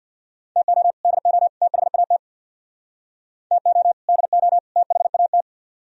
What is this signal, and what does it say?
Morse code, heard as a keyed tone, about 65 wpm, 700 Hz. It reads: TO DO THAT TO DO THAT